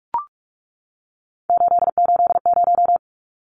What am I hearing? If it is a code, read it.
Morse code, heard as a keyed tone, about 44 wpm, 700 Hz. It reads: 880
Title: Numbers - 3-Digit 45wpm